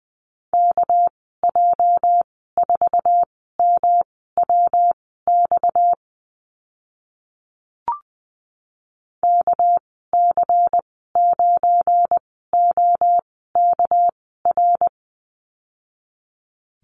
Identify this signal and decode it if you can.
Morse code, heard as a keyed tone, about 20 wpm, spaced as for 14 wpm, 700 Hz. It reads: KJ4MWX KC9OKR
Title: 231 Lesson 37g - Letter x - Call Signs 20wpm